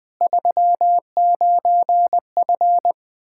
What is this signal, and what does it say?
Morse code, heard as a keyed tone, about 21 wpm, 700 Hz. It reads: 39F
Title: Practice QSO - Exchange 2 - Info - 20wpm